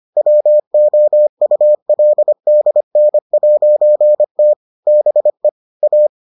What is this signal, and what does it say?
Morse code, heard as a keyed tone, about 25 wpm, 600 Hz. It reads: WOULDN'T BE A